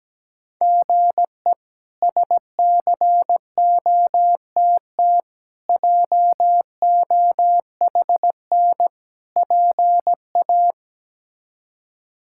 Morse code, heard as a keyed tone, about 17 wpm, 700 Hz. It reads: GE SCOTT JOHN PA
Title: K1USN SST QSO Practice - 17wpm